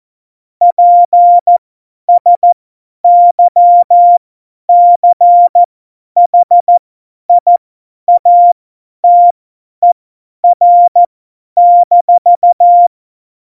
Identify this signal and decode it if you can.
Morse code, heard as a keyed tone, about 14 wpm, spaced as for 10 wpm, 700 Hz. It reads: PSYCHIATER-
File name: Morsekurs / Witze / Witz 23. Januar 2018, 50 F70